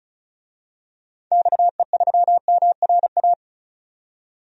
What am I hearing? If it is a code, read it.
Morse code, heard as a keyed tone, about 36 wpm, 700 Hz. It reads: XE3MRA
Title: DXCC-100 Call Signs - 6 Characters - 35wpm